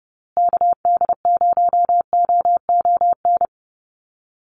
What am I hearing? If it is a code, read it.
Morse code, heard as a keyed tone, about 30 wpm, 700 Hz. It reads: KD0OON
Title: USA Call Signs Repeated 2x 30wpm